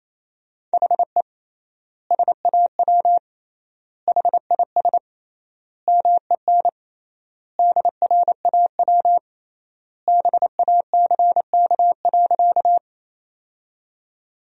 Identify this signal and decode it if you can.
Morse code, heard as a keyed tone, about 28 wpm, 700 Hz. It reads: HE SAW HIS MEN DRAW BACK.
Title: 3x Word-Spacing - Sentences from Top 300 Words - 28wpm